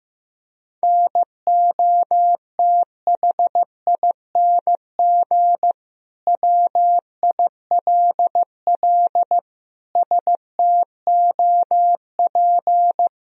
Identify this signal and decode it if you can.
Morse code, heard as a keyed tone, about 16 wpm, 700 Hz. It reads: NOTHING WILL STOP
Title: Sentences from Top 1000 Words - Review Part 1 - 15wpm